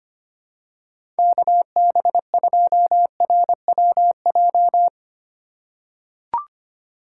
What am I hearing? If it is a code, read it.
Morse code, heard as a keyed tone, about 25 wpm, 700 Hz. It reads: KB2RWJ
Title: USA Call Signs - 6 Characters - 4x Character Spacing - 25wpm